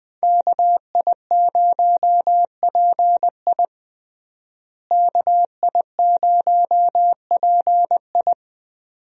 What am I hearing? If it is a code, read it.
Morse code, heard as a keyed tone, about 20 wpm, 700 Hz. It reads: KI0PI KI0PI
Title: USA Call Signs - 5 Characters - Repeated 2x - 20wpm